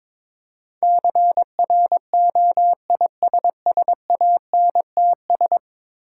Morse code, heard as a keyed tone, about 22 wpm, 690 Hz. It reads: CROISSANTS
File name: Difficult Words with Hints - 22wpm